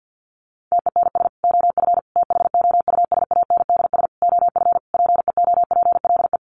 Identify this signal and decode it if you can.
Morse code, heard as a keyed tone, about 50 wpm, 700 Hz. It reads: TENS OF THOUSANDS OF PEOPLE
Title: Sets of 5 Words Encore2 50wpm